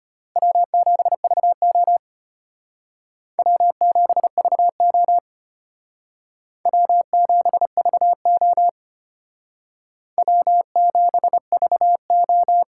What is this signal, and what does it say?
Morse code, heard as a keyed tone, about 32 wpm, 700 Hz. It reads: W7VO W7VO W7VO W7VO